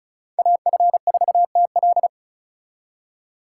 Morse code, heard as a keyed tone, about 44 wpm, 700 Hz. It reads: AF4TL
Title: Speed Racing - USA 2x2 Call Signs 35wpm